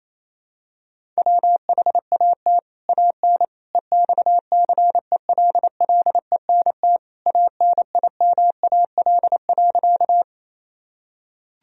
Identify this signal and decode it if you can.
Morse code, heard as a keyed tone, about 28 wpm, 700 Hz. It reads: WHAT AN EXCELLENT ANIMAL.
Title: Sentences from Top 1000 Words - Review Part 1 - 28wpm